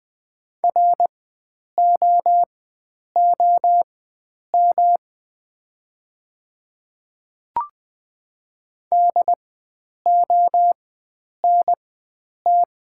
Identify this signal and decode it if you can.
Morse code, heard as a keyed tone, about 20 wpm, spaced as for 9 wpm, 690 Hz. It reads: ROOM DONT
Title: 4x Character Spacing - Top 300 Words - 20wpm